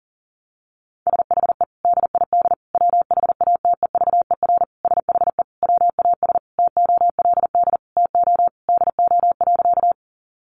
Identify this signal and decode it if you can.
Morse code, heard as a keyed tone, about 37 wpm, 700 Hz. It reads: SHE DID WHATEVER SHE WAS TOLD TO DO.
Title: Sentences from Top 1100 Words 40wpm